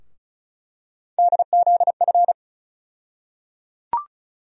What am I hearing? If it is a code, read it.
Morse code, heard as a keyed tone, about 34 wpm, 700 Hz. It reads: DZF
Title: USA Call Sign Postfixes - 3 Letters 35wpm